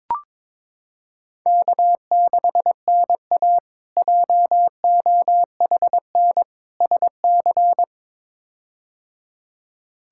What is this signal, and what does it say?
Morse code, heard as a keyed tone, about 22 wpm, 700 Hz. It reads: K6NA JOHN SC
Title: K1USN SST QSO Practice - 22wpm